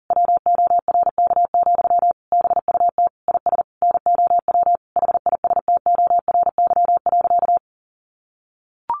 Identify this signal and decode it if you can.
Morse code, heard as a keyed tone, about 40 wpm, 700 Hz. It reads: WORK, BUT IS NOW HISTORY.
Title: Sentences from Top 1500 Words 40wpm